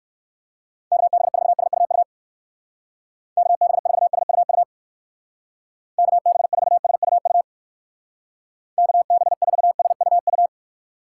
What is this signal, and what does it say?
Morse code, heard as a keyed tone, about 63 wpm, 700 Hz. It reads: KB4SUU KB4SUU KB4SUU KB4SUU